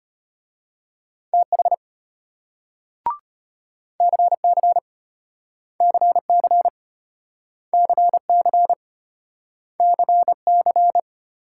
Morse code, heard as a keyed tone, about 33 wpm, 700 Hz. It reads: TH CC CC CC CC